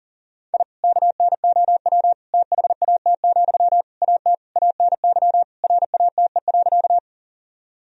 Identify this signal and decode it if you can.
Morse code, heard as a keyed tone, about 40 wpm, 700 Hz. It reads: I KNOW THAT, AT ANY RATE.
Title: Sentences from Top 1000 Words - Review Part 4 - 40wpm